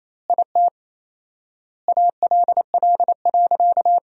Morse code, heard as a keyed tone, about 28 wpm, 700 Hz. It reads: IT ALL.
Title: 4x Word-Spacing - Sentences from Top 300 Words - 28wpm